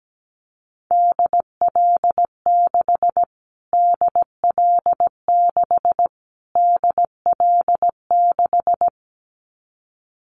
Morse code, heard as a keyed tone, about 17 wpm, 700 Hz. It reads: DL6 DL6 DL6
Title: Callsign Prefixes for Top 25 DXCC - Repeated 3x - 17wpm